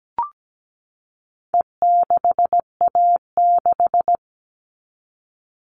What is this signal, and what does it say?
Morse code, heard as a keyed tone, about 17 wpm, 700 Hz. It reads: E6A6
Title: All Cut Numbers - 4-Digit 17wpm